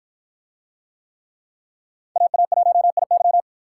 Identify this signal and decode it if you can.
Morse code, heard as a keyed tone, about 53 wpm, 700 Hz. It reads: AA1IY